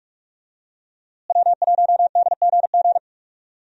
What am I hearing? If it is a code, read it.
Morse code, heard as a keyed tone, about 44 wpm, 700 Hz. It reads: W1DGG
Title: USA Call Signs - 5 Characters - Repeated 3x - 45wpm